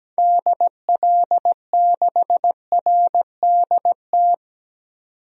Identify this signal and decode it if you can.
Morse code, heard as a keyed tone, about 17 wpm, 700 Hz. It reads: DL6RDT
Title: DXCC-100 Call Signs - 6 Characters - 17wpm